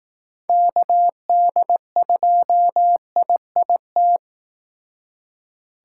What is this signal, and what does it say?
Morse code, heard as a keyed tone, about 18 wpm, 700 Hz. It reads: KD2IIT
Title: Speed Racing - USA 2x3 Call Signs 15wpm